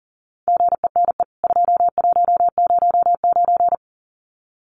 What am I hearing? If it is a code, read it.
Morse code, heard as a keyed tone, about 40 wpm, 700 Hz. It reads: GENE 2109
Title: CWT Exchange 40wpm